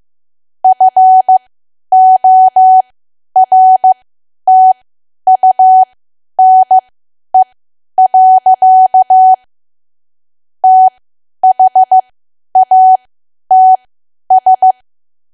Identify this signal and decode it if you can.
Morse code, heard as a keyed tone, about 15 wpm, spaced as for 10 wpm, 750 Hz. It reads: FORTUNE. THATS
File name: cwa / W1AW-Level2 / 10WPM